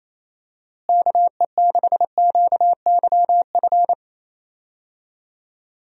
Morse code, heard as a keyed tone, about 27 wpm, 700 Hz. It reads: KE6QYF
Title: USA Call Signs - 6 Characters - Repeated 3x - 28wpm